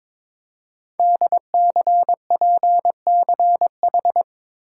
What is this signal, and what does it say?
Morse code, heard as a keyed tone, about 22 wpm, 700 Hz. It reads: DCPCH
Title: Code Groups - 5 Letters - 22wpm